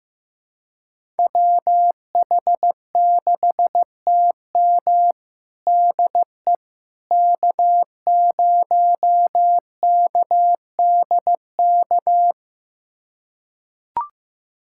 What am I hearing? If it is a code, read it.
Morse code, heard as a keyed tone, about 15 wpm, 700 Hz. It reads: WH6TM DE K0KDK
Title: USA Callsign DE Callsign - 5 Characters - 15wpm